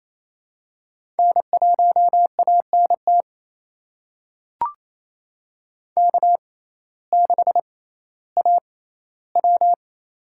Morse code, heard as a keyed tone, about 28 wpm, spaced as for 15 wpm, 700 Hz. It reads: N1ANT K6AW
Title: USA Call Signs - 6x Character Spacing - 28wpm